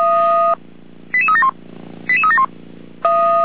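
Five-tone selective call (ZVEI), heard as quick runs of five tones, tones of about 70 ms. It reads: 88371 88371
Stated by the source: ZVEI five-tone selective call 88325, received on SDR.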